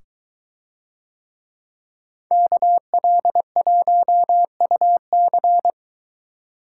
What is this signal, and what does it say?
Morse code, heard as a keyed tone, about 22 wpm, 700 Hz. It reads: KL1UC